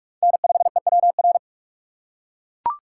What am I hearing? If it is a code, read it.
Morse code, heard as a keyed tone, about 42 wpm, 700 Hz. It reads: N5EWR